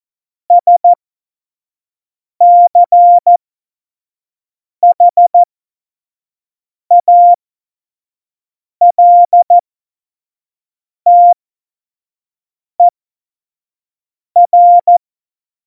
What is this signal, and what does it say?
Morse code, heard as a keyed tone, about 14 wpm, spaced as for 5 wpm, 700 Hz. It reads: SCHALTER
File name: Morsekurs / Witze / Witz 21 April 2020, 25 F70